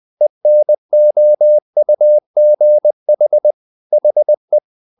Morse code, heard as a keyed tone, about 20 wpm, 600 Hz. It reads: ENOUGH HE
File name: the old man and the sea part 2 20wpm